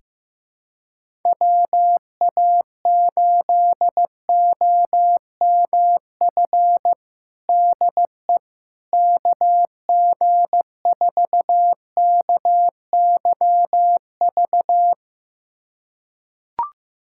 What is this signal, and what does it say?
Morse code, heard as a keyed tone, about 15 wpm, 700 Hz. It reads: WA8OMF DE KG4KYV
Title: USA Callsign DE Callsign - 6 Characters - 15wpm